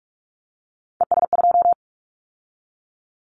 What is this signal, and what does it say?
Morse code, heard as a keyed tone, about 46 wpm, 700 Hz. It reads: ES2
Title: Callsign Prefixes for Top 50 DXCC - 45wpm